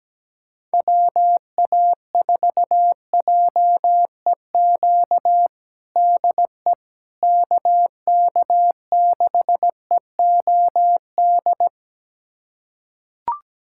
Morse code, heard as a keyed tone, about 17 wpm, 700 Hz. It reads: WA4JEQ DE KK6EOD
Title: USA Callsign DE Callsign 17wpm